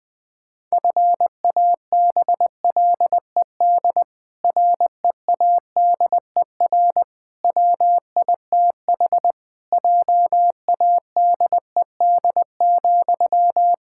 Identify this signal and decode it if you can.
Morse code, heard as a keyed tone, about 19 wpm, 700 Hz. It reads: FABLED READER WITH JADED,
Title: Pangram Challenge v2 - 20wpm